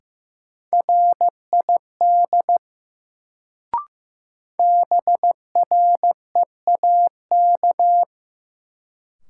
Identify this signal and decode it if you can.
Morse code, heard as a keyed tone, about 15 wpm, 700 Hz. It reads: RID BREAK
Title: Sets of 4 Words - New Word Drill - 15wpm